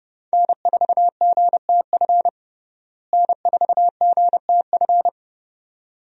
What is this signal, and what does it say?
Morse code, heard as a keyed tone, about 30 wpm, 700 Hz. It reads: N4GTF N4GTF
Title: USA Call Signs - 5 Characters - Repeated 2x - 30wpm